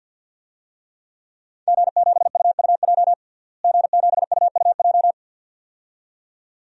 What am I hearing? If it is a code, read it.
Morse code, heard as a keyed tone, about 50 wpm, 700 Hz. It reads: G7UUJ G7UUJ
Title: DXCC-100 Call Signs - 5 Characters - Repeated 2x - 50wpm